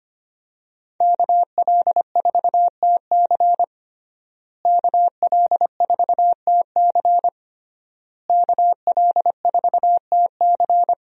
Morse code, heard as a keyed tone, about 25 wpm, 700 Hz. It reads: KL4TC KL4TC KL4TC
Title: USA Call Signs - 5 Characters - Repeated 3x - 25wpm